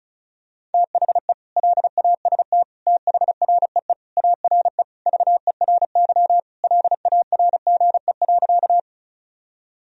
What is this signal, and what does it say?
Morse code, heard as a keyed tone, about 36 wpm, 700 Hz. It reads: THE LAST THREE ARE VERY LARGE.
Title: Sentences from Top 1000 Words - Review Part 1 - 35wpm